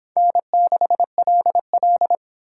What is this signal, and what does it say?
Morse code, heard as a keyed tone, about 30 wpm, 700 Hz. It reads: N6LL